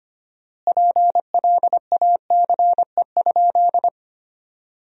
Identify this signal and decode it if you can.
Morse code, heard as a keyed tone, about 25 wpm, 700 Hz. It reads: PLACE?
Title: 4x Word-Spacing - Sentences from Top 200 Words - 25wpm